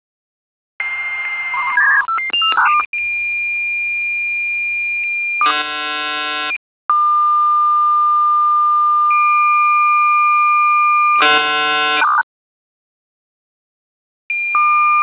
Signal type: dial-up modem handshake